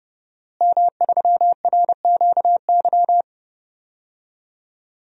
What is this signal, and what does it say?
Morse code, heard as a keyed tone, about 30 wpm, 700 Hz. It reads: M3RQY
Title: DXCC-100 Call Signs - 5 Characters - Repeated 2x - 30wpm